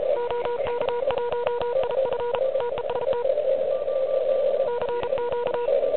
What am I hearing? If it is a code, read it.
Morse code, heard as a keyed tone, about 34 wpm, 960 Hz, with noise. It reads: OK1IF TEST CQ
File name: ARCHIV / QCX / QCX Pileup